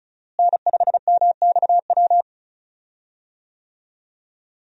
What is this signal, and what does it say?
Morse code, heard as a keyed tone, about 36 wpm, 700 Hz. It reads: N5MXW
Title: USA Call Signs - 5 Characters - Repeated 3x - 35wpm